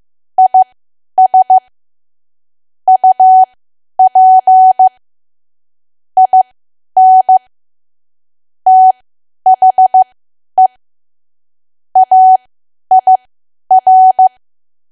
Morse code, heard as a keyed tone, about 15 wpm, spaced as for 10 wpm, 750 Hz. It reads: IS UP IN THE AIR
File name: Archive / 10 WPM / 10WPM